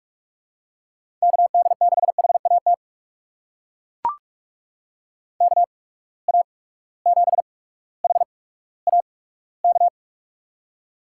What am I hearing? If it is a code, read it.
Morse code, heard as a keyed tone, about 42 wpm, spaced as for 19 wpm, 700 Hz. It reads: KD6HAT KA7HAK
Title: USA Call Signs - 6 Characters - 8x Character Spacing - 45wpm